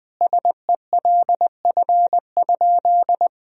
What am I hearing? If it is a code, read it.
Morse code, heard as a keyed tone, about 19 wpm, 700 Hz. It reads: SELF?
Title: Sentences from Top 1000 Words - Review Part 1 - 20wpm